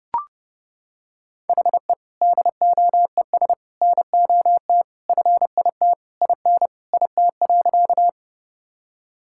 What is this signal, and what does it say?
Morse code, heard as a keyed tone, about 30 wpm, 700 Hz. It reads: HE DOES NOT FIT IN IT.
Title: Sentences from Top 1000 Words - Review Part 5 - 30wpm